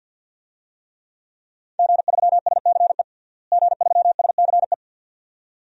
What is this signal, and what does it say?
Morse code, heard as a keyed tone, about 52 wpm, 700 Hz. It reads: G3SCE G3SCE
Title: DXCC-100 Call Signs - 5 Characters - Repeated 2x - 50wpm